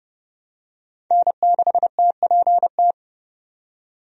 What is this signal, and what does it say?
Morse code, heard as a keyed tone, about 32 wpm, 700 Hz. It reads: N6TPT
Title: USA Call Signs - 5 Characters - 30wpm